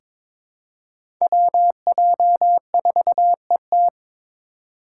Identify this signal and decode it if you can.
Morse code, heard as a keyed tone, about 22 wpm, 700 Hz. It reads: WJ4ET
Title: USA Call Signs - 3x Character Spacing - 22wpm